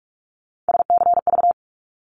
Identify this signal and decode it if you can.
Morse code, heard as a keyed tone, about 46 wpm, 700 Hz. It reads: SCV
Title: ARRL Sections 45wpm